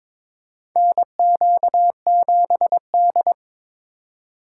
Morse code, heard as a keyed tone, about 22 wpm, 700 Hz. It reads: NQ7D
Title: USA Call Signs - 4 Characters - 3x Character Spacing - 22wpm